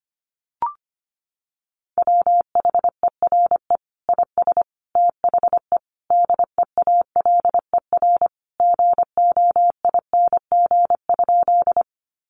Morse code, heard as a keyed tone, about 25 wpm, 700 Hz. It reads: WHERE IS THE DEALER GOING?
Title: Sentences from Top 1100 Words 25wpm